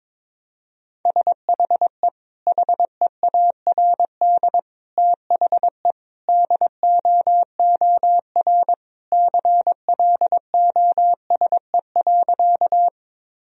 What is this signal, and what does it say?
Morse code, heard as a keyed tone, about 21 wpm, 700 Hz. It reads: SHE HEARD THE DOOR CLOSE.
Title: Sentences from Top 1000 Words - Review Part 1 - 22wpm